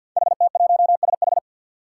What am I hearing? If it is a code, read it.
Morse code, heard as a keyed tone, about 52 wpm, 700 Hz. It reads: HT1SH
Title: Five Challenging Letter-Number Combinations 50wpm